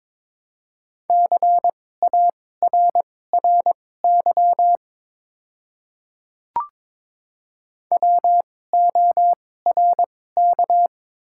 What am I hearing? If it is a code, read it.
Morse code, heard as a keyed tone, about 22 wpm, spaced as for 16 wpm, 690 Hz. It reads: CARRY WORK
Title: 2x Character Spacing - Top 500 Words - 22wpm